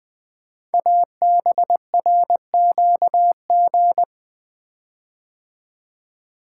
Morse code, heard as a keyed tone, about 20 wpm, 700 Hz. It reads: ABRQG
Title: Code Groups - 5 Letters - 20wpm